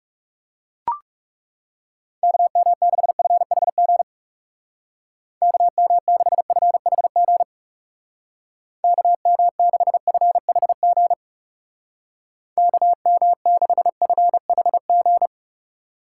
Morse code, heard as a keyed tone, about 37 wpm, 700 Hz. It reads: KM6FHG KM6FHG KM6FHG KM6FHG
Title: Speed Racing - USA 2x3 Call Signs 30wpm